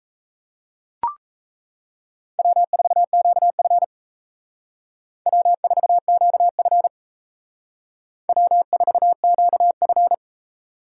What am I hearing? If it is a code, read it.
Morse code, heard as a keyed tone, about 36 wpm, 700 Hz. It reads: W4QF W4QF W4QF